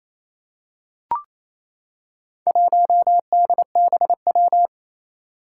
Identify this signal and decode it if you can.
Morse code, heard as a keyed tone, about 29 wpm, 700 Hz. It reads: 1DBW
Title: Four Challenging Letter-Number Combinations 28wpm